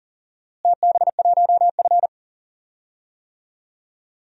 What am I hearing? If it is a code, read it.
Morse code, heard as a keyed tone, about 40 wpm, 700 Hz. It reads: TB1F
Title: DXCC-100 Call Signs - 4 Characters - Repeated 2x - 40wpm